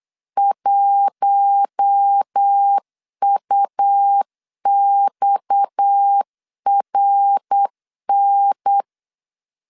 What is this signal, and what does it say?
Morse code, heard as a keyed tone, about 8 wpm, 800 Hz. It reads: 1UXRN
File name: LEZIONI / 6 PKX0 / 6 PKX 8wpm 3041